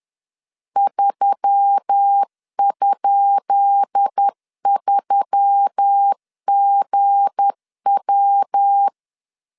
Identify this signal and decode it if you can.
Morse code, heard as a keyed tone, about 10 wpm, 800 Hz. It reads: 3?3GW
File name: LEZIONI / 10 SIGNS / 10B INT 10wpm 2844